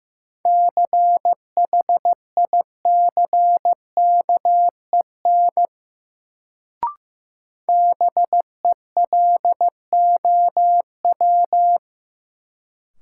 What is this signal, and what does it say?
Morse code, heard as a keyed tone, about 15 wpm, 700 Hz. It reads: CHICKEN BELOW
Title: Sentences from Top 1000 Words - Review Part 5 - New Word Drill - 15wpm